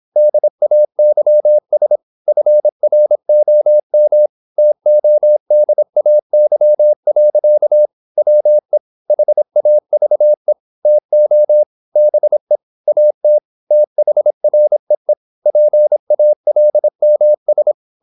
Morse code, heard as a keyed tone, about 26 wpm, 600 Hz. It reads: DAYS FROM TODAY. WE HAVE TO BE AT THREE PALMS
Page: Never 26wpm | CW med Gnister